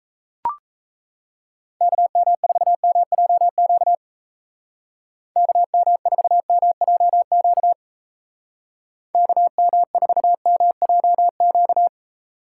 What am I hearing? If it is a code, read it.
Morse code, heard as a keyed tone, about 33 wpm, 700 Hz. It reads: KM4MJQ KM4MJQ KM4MJQ
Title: Speed Racing - USA 2x3 Call Signs 28wpm